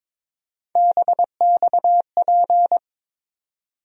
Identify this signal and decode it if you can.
Morse code, heard as a keyed tone, about 22 wpm, 700 Hz. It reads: BXP
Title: USA Call Sign Postfixes - 3 Letters 22wpm